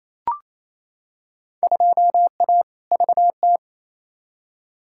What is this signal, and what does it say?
Morse code, heard as a keyed tone, about 28 wpm, 700 Hz. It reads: 2A VT
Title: Field Day Exchanges Encore - 28wpm